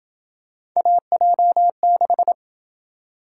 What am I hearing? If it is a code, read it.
Morse code, heard as a keyed tone, about 27 wpm, 700 Hz. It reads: AJ6